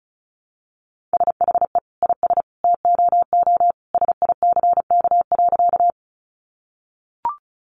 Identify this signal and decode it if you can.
Morse code, heard as a keyed tone, about 34 wpm, 700 Hz. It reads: SHE IS TOO SICK.
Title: Easy Phrases 35wpm